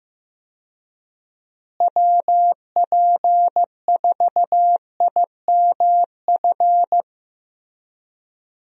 Morse code, heard as a keyed tone, about 15 wpm, 700 Hz. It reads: WP4IMF
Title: USA Call Signs - 6 Characters - Repeated 2x - 15wpm